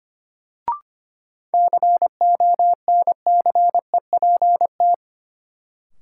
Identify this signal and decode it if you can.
Morse code, heard as a keyed tone, about 25 wpm, 700 Hz. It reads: CONCEPT
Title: Sentences from Top 1100 Words - New Word Drill - 25wpm